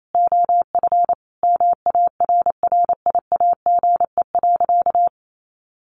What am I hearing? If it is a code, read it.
Morse code, heard as a keyed tone, about 28 wpm, 700 Hz. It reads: OF MARRIAGE.
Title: Sentences from Top 800 Words 28wpm